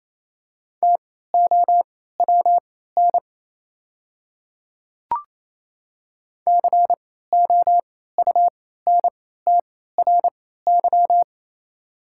Morse code, heard as a keyed tone, about 28 wpm, spaced as for 16 wpm, 700 Hz. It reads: TOWN COUNTRY
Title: 3x Character Spacing - Top 500 Words - 28wpm